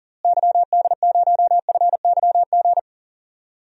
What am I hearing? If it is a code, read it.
Morse code, heard as a keyed tone, about 40 wpm, 700 Hz. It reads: YD0FYG
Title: DXCC-100 Call Signs - 6 Characters - Repeated 2x - 40wpm